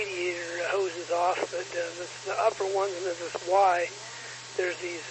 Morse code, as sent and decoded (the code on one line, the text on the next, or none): none